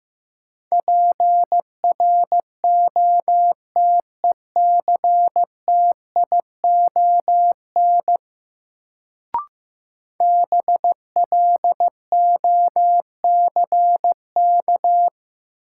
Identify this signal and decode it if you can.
Morse code, heard as a keyed tone, about 15 wpm, 700 Hz. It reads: PROTECTION BLOCK
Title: Sentences from Top 1600 Words - New Word Drill Alt - 15wpm